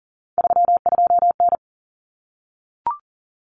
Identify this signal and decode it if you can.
Morse code, heard as a keyed tone, about 40 wpm, 700 Hz. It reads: 32N
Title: Common Cut Numbers - 3-Digit 40wpm